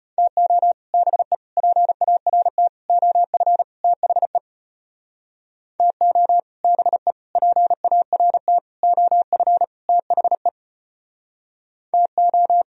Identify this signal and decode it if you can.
Morse code, heard as a keyed tone, about 32 wpm, 700 Hz. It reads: TO BE PART OF THE TO BE PART OF THE TO